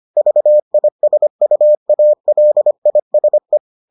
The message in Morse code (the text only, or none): VISUALISE